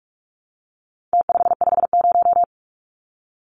Morse code, heard as a keyed tone, about 44 wpm, 700 Hz. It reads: T550
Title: Numbers - Contest Style - 4-Digit 45wpm